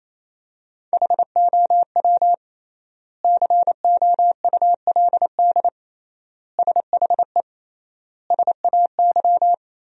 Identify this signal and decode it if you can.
Morse code, heard as a keyed tone, about 28 wpm, 700 Hz. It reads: HOW COULD SHE SAY
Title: 3x Word-Spacing - Sentences from Top 200 Words - 28wpm